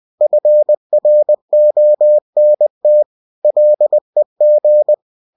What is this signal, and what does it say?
Morse code, heard as a keyed tone, about 20 wpm, 600 Hz. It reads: FRONT LEG
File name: cw-books / winnie-the-pooh / winnie-20wpm-page-0107